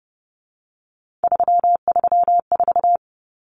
Morse code, heard as a keyed tone, about 30 wpm, 700 Hz. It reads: 334
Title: Common Cut Numbers - 3-Digit 30wpm